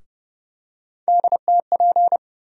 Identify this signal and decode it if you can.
Morse code, heard as a keyed tone, about 30 wpm, 700 Hz. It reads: DTP
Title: USA Call Sign Postfixes - 3 Letters 30wpm